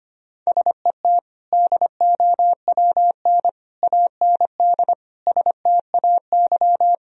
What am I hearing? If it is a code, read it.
Morse code, heard as a keyed tone, about 25 wpm, 700 Hz. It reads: SET DOWN AND STAY
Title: Sentences from Top 1000 Words - Review Part 1 - 25wpm